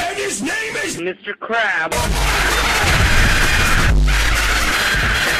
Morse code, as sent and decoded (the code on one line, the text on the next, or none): none